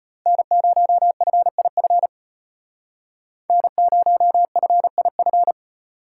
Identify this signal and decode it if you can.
Morse code, heard as a keyed tone, about 33 wpm, 700 Hz. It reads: N0FIF N0FIF